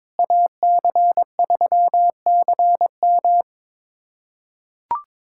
Morse code, heard as a keyed tone, about 22 wpm, 700 Hz. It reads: AC3CM
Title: USA Callsign DE Callsign - 5 Characters - 22wpm